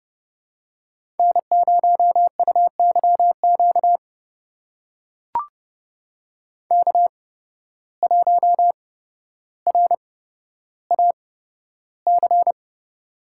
Morse code, heard as a keyed tone, about 29 wpm, spaced as for 13 wpm, 700 Hz. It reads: N0UYQ K1RAC
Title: USA Call Signs - 5 Characters - 8x Character Spacing - 30wpm